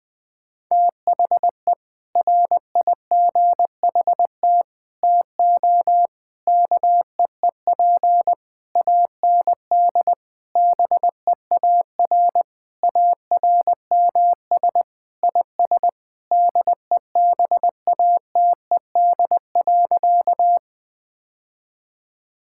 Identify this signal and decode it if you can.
Morse code, heard as a keyed tone, about 20 wpm, 700 Hz. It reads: THE RIGHT TO KEEP AND BEAR ARMS IS DEBATED.
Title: Sentences with 7-Word Phrases - 20wpm